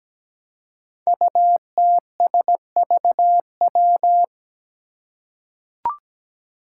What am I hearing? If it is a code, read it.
Morse code, heard as a keyed tone, about 17 wpm, 700 Hz. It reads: UTSVW
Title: Five Challenging Letter-Number Combinations 17wpm